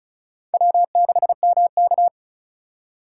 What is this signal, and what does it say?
Morse code, heard as a keyed tone, about 34 wpm, 700 Hz. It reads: W6MK
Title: USA Call Signs - 4 Characters - 2x Character Spacing - 35wpm